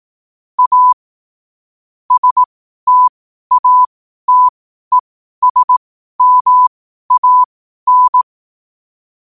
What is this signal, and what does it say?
Morse code, heard as a keyed tone, about 18 wpm, spaced as for 12 wpm, 1000 Hz. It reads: A STATESMAN
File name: qotdfiles / 12wpm